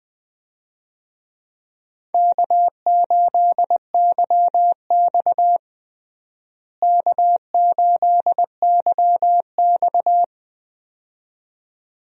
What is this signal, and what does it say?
Morse code, heard as a keyed tone, about 20 wpm, 700 Hz. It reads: K8YX K8YX
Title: USA Call Signs - 4 Characters - Repeated 2x - 20wpm